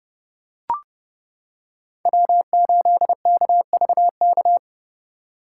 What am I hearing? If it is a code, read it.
Morse code, heard as a keyed tone, about 29 wpm, 700 Hz. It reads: W8KVK